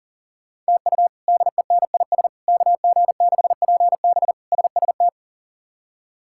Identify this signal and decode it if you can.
Morse code, heard as a keyed tone, about 40 wpm, 700 Hz. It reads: TU DENIS KG6PB SST
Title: K1USN SST QSO Practice - 40wpm